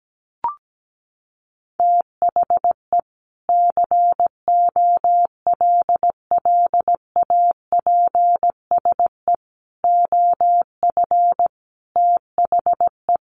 Morse code, heard as a keyed tone, about 17 wpm, 700 Hz. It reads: THE COLLAPSE OF THE
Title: Sets of 5 Words Encore 17wpm